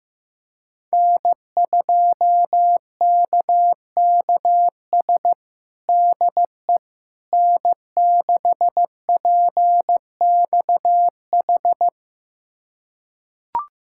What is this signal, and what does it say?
Morse code, heard as a keyed tone, about 15 wpm, 700 Hz. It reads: N2KKS DE N6PXH
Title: USA Callsign DE Callsign - 5 Characters - 15wpm